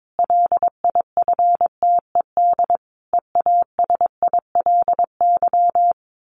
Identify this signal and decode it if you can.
Morse code, heard as a keyed tone, about 22 wpm, 700 Hz. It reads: LIFTED EASILY
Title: A Sound Of Thunder 22wpm